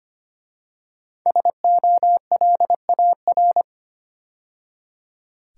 Morse code, heard as a keyed tone, about 25 wpm, 700 Hz. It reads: SOLAR
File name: 5 Letter Words - First 3 Letters Given - 25wpm